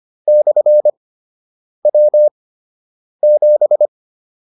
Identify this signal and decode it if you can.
Morse code, heard as a keyed tone, about 25 wpm, 600 Hz. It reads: / W 7